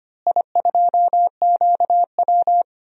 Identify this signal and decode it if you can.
Morse code, heard as a keyed tone, about 25 wpm, 700 Hz. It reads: I2QW
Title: DXCC-100 Call Signs - 4 Characters - Repeated 3x - 25wpm